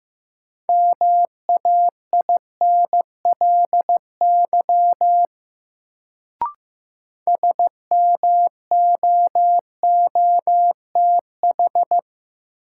Morse code, heard as a keyed tone, about 16 wpm, 700 Hz. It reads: MAINLY SMOOTH
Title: Sentences from Top 1400 Words - New Word Drill - 15wpm